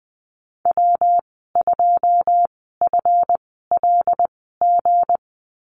Morse code, heard as a keyed tone, about 20 wpm, spaced as for 14 wpm, 700 Hz. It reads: W2FLG
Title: 161 Lesson 26g - Letter g - Call Signs 20wpm